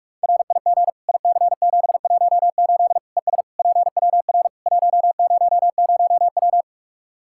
Morse code, heard as a keyed tone, about 46 wpm, 700 Hz. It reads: RIG IC718 ES PWR 100W
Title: Practice QSO - Exchange 2 - Info - 45wpm